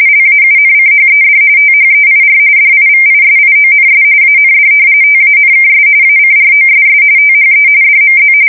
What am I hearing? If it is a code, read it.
RTTY, about 45.45 baud. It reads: LAUNCHED THEIR DIVISIONS AND TANKS ACROSS THE DMZ
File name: fsk170 / battleshipnj